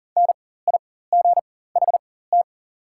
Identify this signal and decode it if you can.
Morse code, heard as a keyed tone, about 40 wpm, spaced as for 19 wpm, 700 Hz. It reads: NIGHT
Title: 4x Character Spacing - Top 300 Words - 40wpm